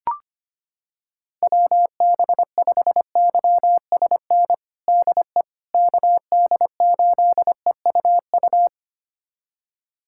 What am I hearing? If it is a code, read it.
Morse code, heard as a keyed tone, about 25 wpm, 700 Hz. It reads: WB5YSN DE KD8EUU
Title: USA Callsign DE Callsign - 6 Characters - 25wpm